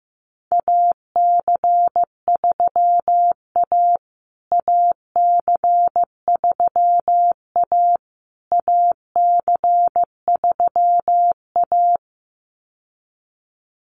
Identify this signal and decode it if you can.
Morse code, heard as a keyed tone, about 15 wpm, 700 Hz. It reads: AC3A AC3A AC3A
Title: USA Call Signs Repeated 3x 15wpm